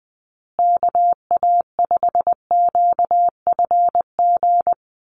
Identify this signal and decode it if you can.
Morse code, heard as a keyed tone, about 20 wpm, 700 Hz. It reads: KA5QFG
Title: USA Call Signs Repeated 3x 20wpm